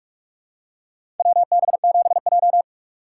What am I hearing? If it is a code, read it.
Morse code, heard as a keyed tone, about 46 wpm, 700 Hz. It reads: WB7J
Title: USA Call Signs - 4 Characters - 4x Character Spacing - 45wpm